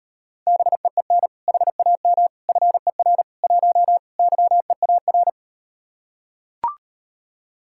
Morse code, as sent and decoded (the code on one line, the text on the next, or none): -... . -. / .... .- -- / ..-. .-. / .---- / -.-- . .- .-.
BEEN HAM FER 1 YEAR